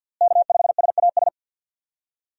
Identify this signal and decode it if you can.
Morse code, heard as a keyed tone, about 52 wpm, 700 Hz. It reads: K5SAS
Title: USA Call Signs - 5 Characters - Repeated 3x - 50wpm